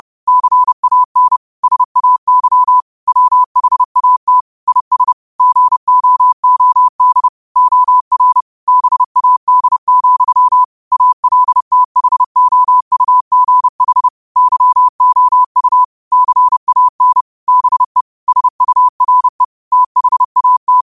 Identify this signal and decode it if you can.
Morse code, heard as a keyed tone, about 30 wpm, 1000 Hz. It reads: CAN SAY WHAT IS GOOD OR BAD, ALTHOUGH YOU CAN BE SURE THAT
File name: qotdfiles / 30wpm